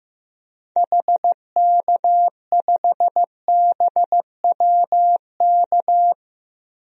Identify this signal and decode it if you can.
Morse code, heard as a keyed tone, about 15 wpm, 700 Hz. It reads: HK5BWK
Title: DXCC-100 Call Signs - 6 Characters - Repeated 3x - 15wpm